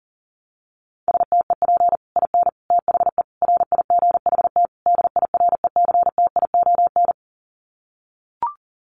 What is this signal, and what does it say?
Morse code, heard as a keyed tone, about 40 wpm, 700 Hz. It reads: STEP IN THE RIGHT DIRECTION
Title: Sets of 5 Words Encore 40wpm